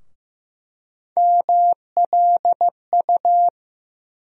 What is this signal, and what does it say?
Morse code, heard as a keyed tone, about 15 wpm, 700 Hz. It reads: MLU
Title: USA Call Sign Postfixes - 3 Letters 15wpm